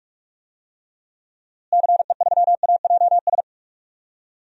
Morse code, heard as a keyed tone, about 43 wpm, 700 Hz. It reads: CE3AJS
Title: DXCC-100 Call Signs - 6 Characters - Repeated 3x - 45wpm